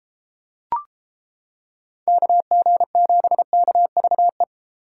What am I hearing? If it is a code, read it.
Morse code, heard as a keyed tone, about 32 wpm, 700 Hz. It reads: KG7KVE